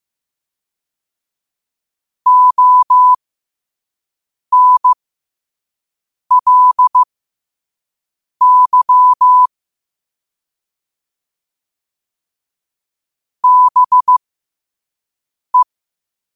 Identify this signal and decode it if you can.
Morse code, heard as a keyed tone, about 15 wpm, spaced as for 5 wpm, 1000 Hz. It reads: ONLY BE
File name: qotdfiles / 5wpm